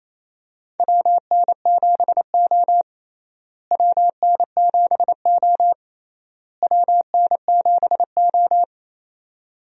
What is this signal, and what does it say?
Morse code, heard as a keyed tone, about 28 wpm, 700 Hz. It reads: WN7O WN7O WN7O
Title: USA Call Signs - 4 Characters - Repeated 3x - 28wpm